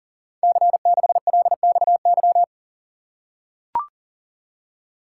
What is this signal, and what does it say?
Morse code, heard as a keyed tone, about 43 wpm, 700 Hz. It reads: C6LXY
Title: Code Groups - 5 Letter:Numbers - 40wpm